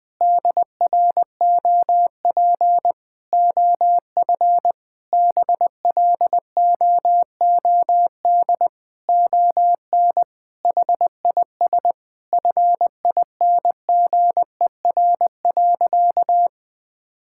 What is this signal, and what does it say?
Morse code, heard as a keyed tone, about 20 wpm, 700 Hz. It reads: DROP OF BLOOD ON HIS FINGER.
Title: Sentences from Top 1600 Words 20wpm